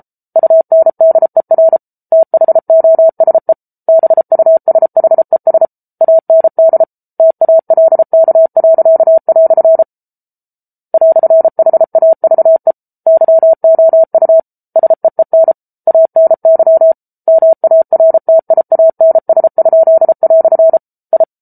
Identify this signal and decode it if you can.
Morse code, heard as a keyed tone, about 33 wpm, 650 Hz. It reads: UNDER THOSE BUSHES AND TALK." "HAVE YOU SEEN ANY MARTIANS?" I